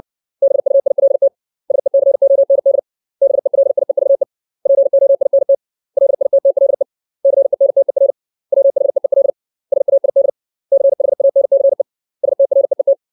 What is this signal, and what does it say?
Morse code, heard as a keyed tone, about 60 wpm, 550 Hz. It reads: BUILT SECOND BELIEVE POINT LITTLE CENTER WHEEL SAID CHANCE STREET
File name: 632 words / 632 Words 60WPM